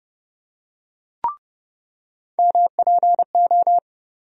Morse code, heard as a keyed tone, about 32 wpm, 700 Hz. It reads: MPO
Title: USA Call Sign Postfixes - 3 Letters 30wpm